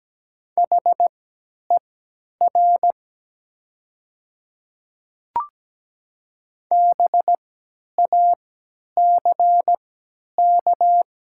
Morse code, heard as a keyed tone, about 17 wpm, spaced as for 10 wpm, 700 Hz. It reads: HER BACK
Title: 3x Character Spacing - Top 200 Words - 17wpm